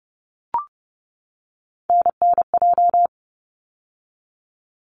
Morse code, heard as a keyed tone, about 31 wpm, 700 Hz. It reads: NNJ
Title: ARRL Sections 30wpm